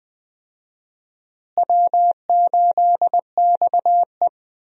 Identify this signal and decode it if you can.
Morse code, heard as a keyed tone, about 20 wpm, 700 Hz. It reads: W8XE
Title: USA Call Signs - 4 Characters - Repeated 2x - 20wpm